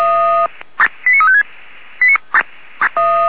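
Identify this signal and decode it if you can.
Five-tone selective call (ZVEI), heard as quick runs of five tones, tones of about 70 ms. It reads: 89356
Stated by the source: ZVEI five-tone call received by SDR, from the feuerwehr-orlamuende.org (f85360) audio archive.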